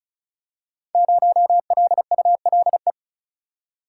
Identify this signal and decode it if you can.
Morse code, heard as a keyed tone, about 36 wpm, 700 Hz. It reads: 0LULE
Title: Code Groups - 5 Letter:Numbers - 35wpm